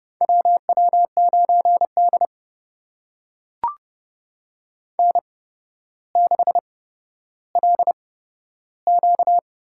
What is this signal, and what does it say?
Morse code, heard as a keyed tone, about 30 wpm, spaced as for 13 wpm, 700 Hz. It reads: WW9D N6LQ
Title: USA Call Signs - 4 Characters - 8x Character Spacing - 30wpm